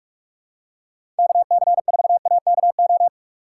Morse code, heard as a keyed tone, about 44 wpm, 700 Hz. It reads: KC4AKO
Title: USA Call Signs - 6 Characters - 8x Character Spacing - 45wpm